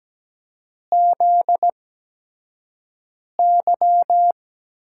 Morse code, heard as a keyed tone, about 17 wpm, spaced as for 5 wpm, 700 Hz. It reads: ZY
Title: USA Call Signs - 4 Characters - 8x Character Spacing - 17wpm